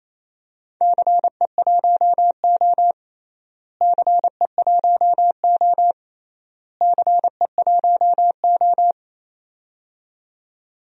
Morse code, heard as a keyed tone, about 28 wpm, 700 Hz. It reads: CE1O CE1O CE1O
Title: DXCC-100 Call Signs - 4 Characters - Repeated 3x - 28wpm